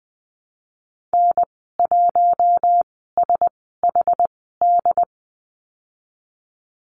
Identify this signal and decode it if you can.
Morse code, heard as a keyed tone, about 20 wpm, spaced as for 15 wpm, 700 Hz. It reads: N1SHD
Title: 193 Lesson 31g - Letter v - Call Signs 20wpm